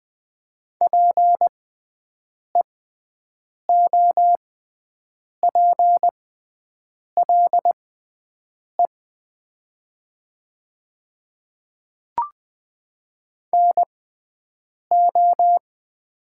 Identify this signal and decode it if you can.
Morse code, heard as a keyed tone, about 20 wpm, spaced as for 7 wpm, 700 Hz. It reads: PEOPLE NO
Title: 6x Character Spacing - Top 100 Words - 20wpm